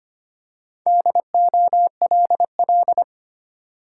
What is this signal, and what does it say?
Morse code, heard as a keyed tone, about 25 wpm, 690 Hz. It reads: DOLL
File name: practice / 25wpm / Easy Words with Hints - 25wpm